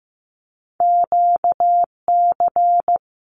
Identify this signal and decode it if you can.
Morse code, heard as a keyed tone, about 15 wpm, 700 Hz. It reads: QC
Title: Canadian Provinces and Territories 15wpm